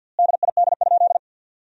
Morse code, heard as a keyed tone, about 47 wpm, 700 Hz. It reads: DID?
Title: 3x Word-Spacing - Sentences from Top 100 Words - 50wpm